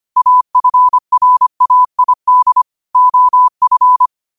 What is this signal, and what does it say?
Morse code, heard as a keyed tone, about 25 wpm, 1000 Hz. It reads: AFRAID OF
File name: qotdfiles / 25wpm